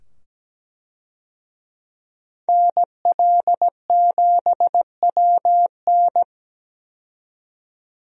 Morse code, heard as a keyed tone, about 17 wpm, 700 Hz. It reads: NL7WN
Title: USA Call Signs - 5 Characters - 17wpm